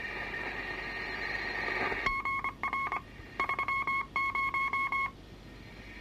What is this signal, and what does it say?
Morse code, heard as a keyed tone, about 24 wpm, 1090 Hz. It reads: GR 30
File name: wrct-log hi- 22